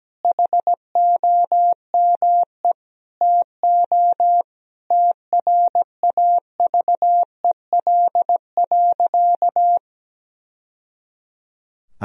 Morse code, heard as a keyed tone, about 17 wpm, 700 Hz. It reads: HOME TO TRAVEL.
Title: Sentences from Top 1000 Words - Review Part 2 - 17wpm